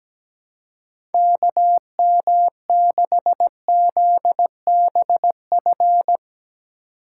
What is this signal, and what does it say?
Morse code, heard as a keyed tone, about 18 wpm, 700 Hz. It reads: KM6ZBF
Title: Speed Racing - USA 2x3 Call Signs 17wpm